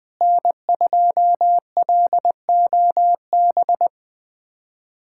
Morse code, heard as a keyed tone, about 20 wpm, 700 Hz. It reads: N2LOB
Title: USA Callsign DE Callsign - 5 Characters - 20wpm